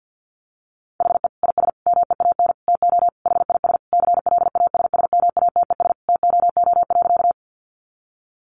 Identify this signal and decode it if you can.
Morse code, heard as a keyed tone, about 48 wpm, 700 Hz. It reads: HE IS MEAN TO HIS CLASSMATES TOO.
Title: Additional Easy Phrases 50wpm